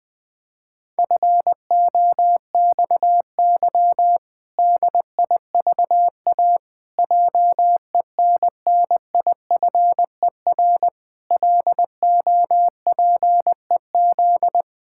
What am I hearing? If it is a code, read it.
Morse code, heard as a keyed tone, about 20 wpm, 700 Hz. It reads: FOXY DIVA JENNIFER LOPEZ
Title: Pangram Challenge v2 - 20wpm